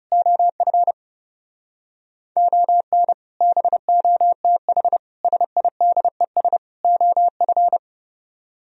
Morse code, heard as a keyed tone, about 38 wpm, 700 Hz. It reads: OF ON BOTH SIDES OF